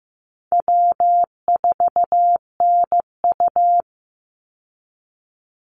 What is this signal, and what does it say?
Morse code, heard as a keyed tone, about 15 wpm, 700 Hz. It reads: W4NU
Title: Easy Call Signs Repeated 3x - 15wpm